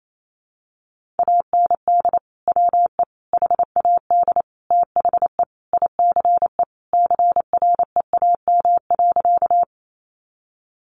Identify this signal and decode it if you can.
Morse code, heard as a keyed tone, about 28 wpm, 700 Hz. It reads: AND WE HAD THE ICE CREAM.
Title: Sentences from Top 1500 Words 28wpm